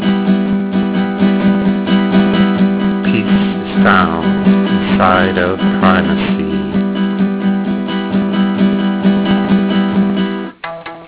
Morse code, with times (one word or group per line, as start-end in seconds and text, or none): none